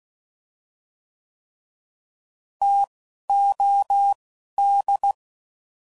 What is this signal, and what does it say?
Morse code, heard as a keyed tone, about 16 wpm, 800 Hz. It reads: T O D